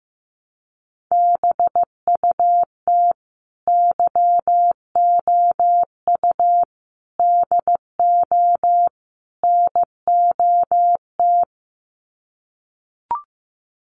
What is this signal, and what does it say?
Morse code, heard as a keyed tone, about 15 wpm, 700 Hz. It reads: BUT YOU DO NOT
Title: Sets of 4 Words 15wpm